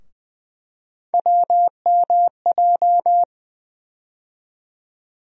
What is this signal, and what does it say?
Morse code, heard as a keyed tone, about 20 wpm, 700 Hz. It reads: WMJ
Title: USA Call Sign Postfixes - 3 Letters 20wpm